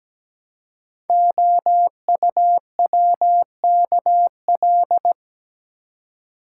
Morse code, heard as a keyed tone, about 17 wpm, 700 Hz. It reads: OUWKL
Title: Code Groups - 5 Letters - 17wpm